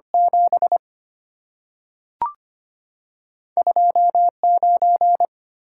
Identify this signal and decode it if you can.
Morse code, heard as a keyed tone, about 25 wpm, 700 Hz. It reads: 7 2 9
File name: Two Letter-Number Combinations - 25wpm